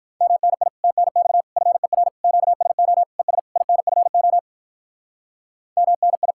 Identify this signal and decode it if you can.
Morse code, heard as a keyed tone, about 42 wpm, 700 Hz. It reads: MNI TNX FER QSO ES INFO MNI
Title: Speed-Racing Common QSO Phrases - 35wpm